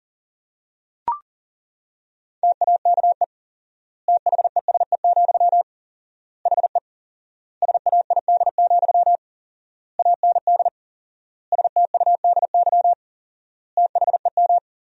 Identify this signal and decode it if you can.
Morse code, heard as a keyed tone, about 37 wpm, 700 Hz. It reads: TAKE THESE, HE SAID, AND STUDY THEM
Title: 4x Word-Spacing - Sentences from Top 300 Words - 40wpm